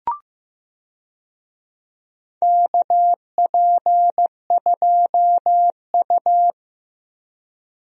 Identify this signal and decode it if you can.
Morse code, heard as a keyed tone, about 15 wpm, 700 Hz. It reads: KP2U
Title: DXCC-100 Call Signs - 4 Characters - Repeated 2x - 15wpm